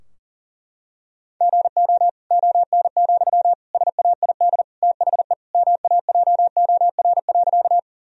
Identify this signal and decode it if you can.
Morse code, heard as a keyed tone, about 40 wpm, 700 Hz. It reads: GO ON, SAID THE MAJOR.
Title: Sentences from Top 1000 Words - Review Part 4 - 40wpm